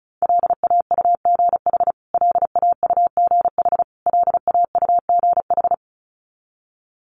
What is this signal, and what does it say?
Morse code, heard as a keyed tone, about 34 wpm, 700 Hz. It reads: LAUGH LAUGH LAUGH
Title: Top 500 Words Repeated 35wpm